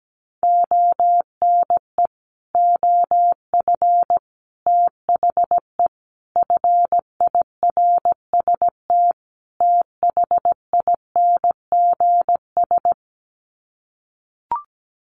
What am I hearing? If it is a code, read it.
Morse code, heard as a keyed tone, about 17 wpm, 700 Hz. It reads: ONE OF THE FIRST THINGS
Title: Sets of 5 Words Encore 17wpm